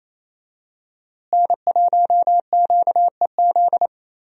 Morse code, heard as a keyed tone, about 28 wpm, 700 Hz. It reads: N1QEZ
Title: USA Call Signs - 5 Characters - Repeated 3x - 28wpm